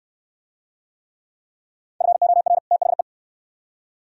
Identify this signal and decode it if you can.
Morse code, heard as a keyed tone, about 63 wpm, 700 Hz. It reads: FOR THE